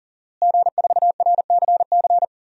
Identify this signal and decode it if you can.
Morse code, heard as a keyed tone, about 37 wpm, 700 Hz. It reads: G4RCC
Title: DXCC-100 Call Signs - 5 Characters - Repeated 2x - 40wpm